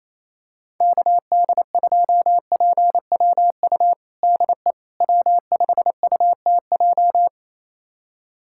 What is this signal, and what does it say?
Morse code, heard as a keyed tone, about 28 wpm, 700 Hz. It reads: KD2PWU DE W5UTJ
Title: USA Callsign DE Callsign 28wpm